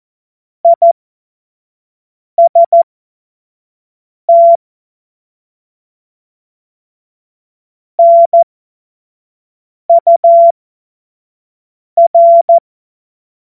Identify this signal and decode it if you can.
Morse code, heard as a keyed tone, about 14 wpm, spaced as for 5 wpm, 680 Hz. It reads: IST NUR
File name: Morsekurs / Witze / Witz 15. November 2022 25 F70